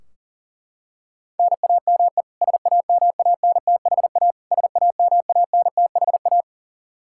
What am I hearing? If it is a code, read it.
Morse code, heard as a keyed tone, about 40 wpm, 700 Hz. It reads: NAME SAMANTHA SAMANTHA
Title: Practice QSO - Exchange 1 - Beginning - 40wpm